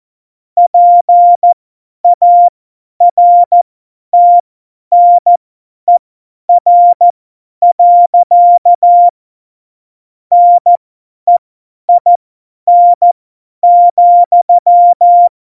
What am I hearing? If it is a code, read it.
Morse code, heard as a keyed tone, about 14 wpm, spaced as for 10 wpm, 700 Hz. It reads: PARTNER. NEIN,
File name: Morsekurs / Witze / Witz 26. Januar 2021, 50 F70